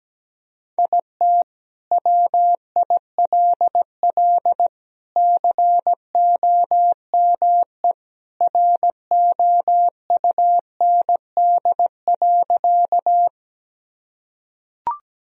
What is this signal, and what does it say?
Morse code, heard as a keyed tone, about 17 wpm, 700 Hz. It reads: IT WILL COME ROUND.
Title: Sentences from Top 1000 Words - Review Part 1 - 17wpm